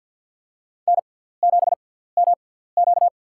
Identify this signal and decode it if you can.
Morse code, heard as a keyed tone, about 50 wpm, spaced as for 17 wpm, 700 Hz. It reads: N7MQ
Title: USA Call Signs - 4 Characters - 6x Character Spacing - 50wpm